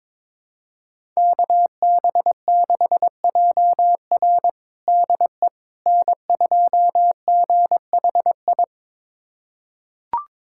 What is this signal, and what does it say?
Morse code, heard as a keyed tone, about 22 wpm, 700 Hz. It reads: KB6JR DE N2GHI
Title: USA Callsign DE Callsign 22wpm